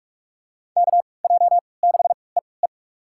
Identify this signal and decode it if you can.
Morse code, heard as a keyed tone, about 41 wpm, spaced as for 25 wpm, 700 Hz. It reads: KJ6EE
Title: USA Call Signs - 5 Characters - 3x Character Spacing - 45wpm